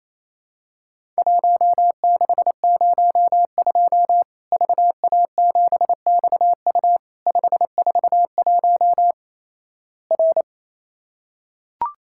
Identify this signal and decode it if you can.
Morse code, heard as a keyed tone, about 28 wpm, 700 Hz. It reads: 1602 VA7XU 541 R
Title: Worked All Europe DX Contest - QTC Exchange - 28wpm